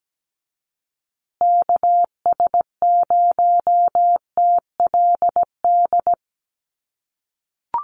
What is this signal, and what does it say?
Morse code, heard as a keyed tone, about 17 wpm, 700 Hz. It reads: KS0TLD
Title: USA Call Signs Repeated 3x 17wpm